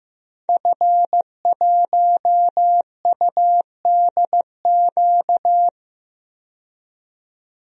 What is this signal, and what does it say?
Morse code, heard as a keyed tone, about 15 wpm, 700 Hz. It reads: F1UDQ
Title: DXCC-100 Call Signs - 5 Characters - 15wpm